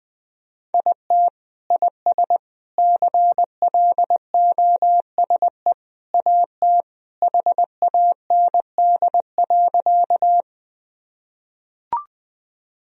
Morse code, heard as a keyed tone, about 20 wpm, 700 Hz. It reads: IT IS CLOSE AT HAND.
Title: Sentences from Top 1000 Words - Review Part 1 - 20wpm